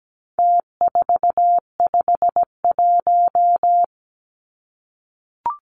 Morse code, heard as a keyed tone, about 17 wpm, 700 Hz. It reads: T451
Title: Numbers - Contest Style - 4-Digit 17wpm